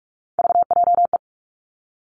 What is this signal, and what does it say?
Morse code, heard as a keyed tone, about 46 wpm, 700 Hz. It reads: VJE
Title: Three Challenging Letter-Number Combinations 45wpm